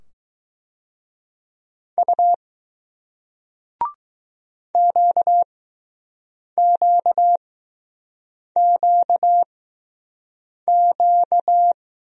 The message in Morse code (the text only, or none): U QQQQ